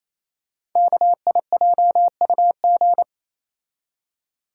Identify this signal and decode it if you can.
Morse code, heard as a keyed tone, about 27 wpm, 700 Hz. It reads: KIJUG
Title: Code Groups - 5 Letters - 28wpm